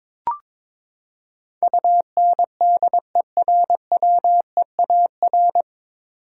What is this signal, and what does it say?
Morse code, heard as a keyed tone, about 22 wpm, 700 Hz. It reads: UNDERWEAR
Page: Morse Code Ninja practice 22wpm